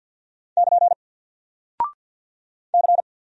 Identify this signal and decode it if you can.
Morse code, heard as a keyed tone, about 49 wpm, 700 Hz. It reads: <KN> C
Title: Single Character (Letter:Number:Punctuation:Prosign) 50wpm